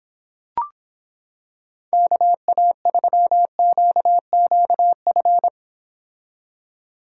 Morse code, heard as a keyed tone, about 26 wpm, 700 Hz. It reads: KA3QQF